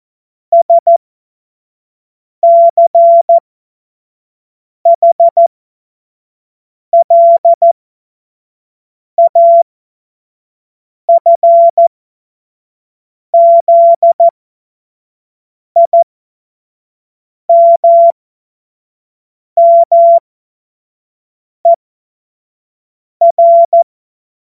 Morse code, heard as a keyed tone, about 14 wpm, spaced as for 5 wpm, 680 Hz. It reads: SCHLAFZIMMER
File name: Morsekurs / Witze / Witz 15. Februar 2022 25 F70